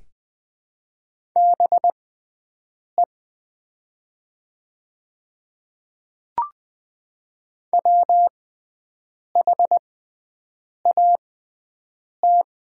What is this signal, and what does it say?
Morse code, heard as a keyed tone, about 20 wpm, spaced as for 7 wpm, 700 Hz. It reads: BE WHAT
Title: 6x Character Spacing - Top 200 Words - 20wpm